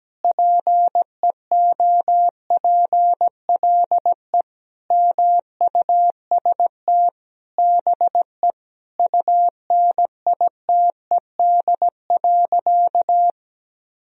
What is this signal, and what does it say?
Morse code, heard as a keyed tone, about 17 wpm, 700 Hz. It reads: PEOPLE MUST BE UNITED.
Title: Sentences from Top 1000 Words - Review Part 4 - 17wpm